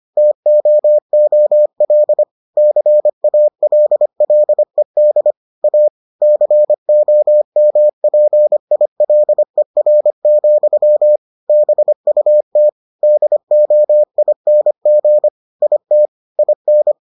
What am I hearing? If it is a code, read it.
Morse code, heard as a keyed tone, about 25 wpm, 600 Hz. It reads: TOOL CALLED A COMPILER, BUT DOING IT IN